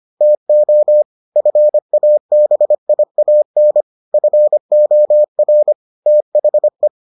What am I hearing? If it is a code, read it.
Morse code, heard as a keyed tone, about 25 wpm, 600 Hz. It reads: TO FABIAN FOR THE